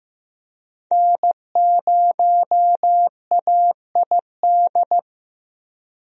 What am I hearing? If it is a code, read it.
Morse code, heard as a keyed tone, about 15 wpm, 700 Hz. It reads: N0AID
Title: USA Call Signs - 5 Characters - Repeated 2x - 15wpm